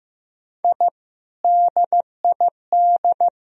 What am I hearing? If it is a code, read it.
Morse code, heard as a keyed tone, about 15 wpm, 700 Hz. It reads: I DID
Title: Sentences from Top 1000 Words - Review Part 1 - 15wpm